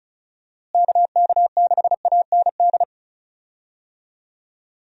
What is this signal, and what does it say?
Morse code, heard as a keyed tone, about 36 wpm, 700 Hz. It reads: KK6AND
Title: USA Call Signs - 6 Characters - Repeated 2x - 35wpm